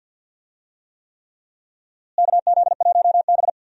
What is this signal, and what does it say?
Morse code, heard as a keyed tone, about 46 wpm, 700 Hz. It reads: KZ1B
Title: USA Call Signs - 4 Characters - Repeated 3x - 50wpm